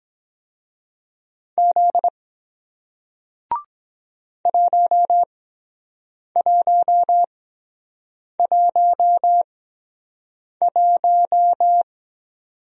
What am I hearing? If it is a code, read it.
Morse code, heard as a keyed tone, about 23 wpm, spaced as for 7 wpm, 700 Hz. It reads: Z 1111